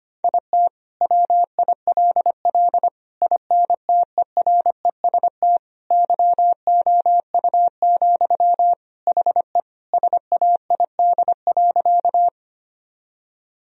Morse code, heard as a keyed tone, about 25 wpm, 700 Hz. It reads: IT WILL INTEREST YOU, HE SAID.
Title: Sentences from Top 1000 Words - Review Part 2 - 25wpm